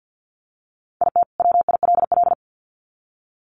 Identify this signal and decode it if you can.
Morse code, heard as a keyed tone, about 50 wpm, 700 Hz. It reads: IT WILL
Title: Sets of 2 Words 50wpm